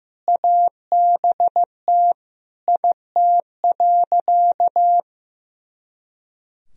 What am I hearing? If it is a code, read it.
Morse code, heard as a keyed tone, about 16 wpm, 700 Hz. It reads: ABT IT.
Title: Sentences from Top 200 Words and Abbreviations 15wpm